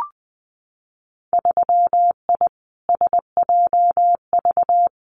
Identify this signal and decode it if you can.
Morse code, heard as a keyed tone, about 20 wpm, 700 Hz. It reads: 3I SJV
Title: Winter Field Day Exchanges Encore 20wpm